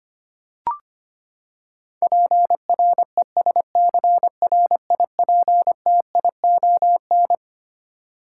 Morse code, heard as a keyed tone, about 25 wpm, 700 Hz. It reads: PRESCRIPTION
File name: practice / 25wpm / Twelve Letter Words - 25wpm